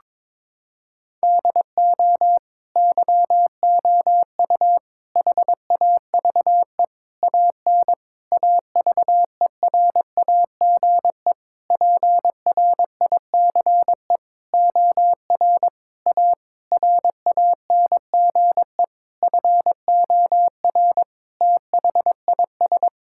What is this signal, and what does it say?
Morse code, heard as a keyed tone, about 21 wpm, 700 Hz. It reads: DO YOU HAVE AN AVERAGE PRICE OR A RANGE FOR THIS
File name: Sentences with 9-Word Phrases - 22wpm